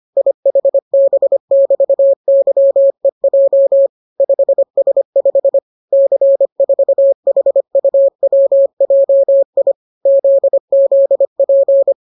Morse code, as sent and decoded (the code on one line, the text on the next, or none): .. .... -... -...- -.-- . .--- / ..... ... ..... / -.-. ....- .... ..- .-- .--- .. / --.. --.. .--.
IHB<BT>YEJ 5S5 C4HUWJI ZZP